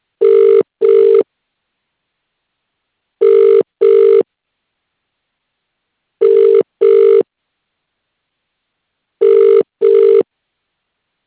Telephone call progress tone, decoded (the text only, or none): ringback tone